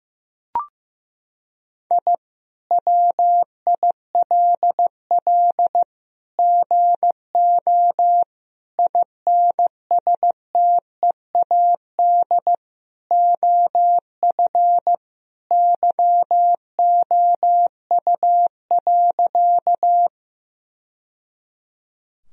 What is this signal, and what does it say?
Morse code, heard as a keyed tone, about 15 wpm, 700 Hz. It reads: I WILL GO INSTEAD OF YOU.
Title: Sentences from Top 1000 Words - Review Part 4 - 15wpm